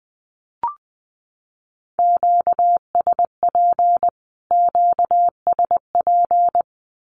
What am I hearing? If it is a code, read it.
Morse code, heard as a keyed tone, about 20 wpm, 700 Hz. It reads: QSP QSP
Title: Q-Codes Repeated 3x 20wpm